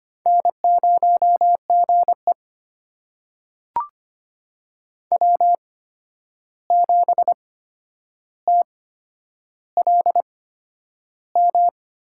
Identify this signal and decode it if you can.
Morse code, heard as a keyed tone, about 25 wpm, spaced as for 10 wpm, 700 Hz. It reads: N0GE W7TLM
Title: USA Call Signs - 8x Character Spacing - 25wpm